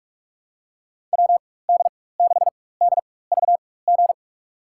Morse code, heard as a keyed tone, about 46 wpm, spaced as for 21 wpm, 700 Hz. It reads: WD6DVG
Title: USA Call Signs - 6 Characters - 4x Character Spacing - 45wpm